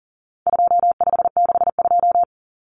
Morse code, heard as a keyed tone, about 40 wpm, 700 Hz. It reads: 2562
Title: Common Cut Numbers - 4-Digit 40wpm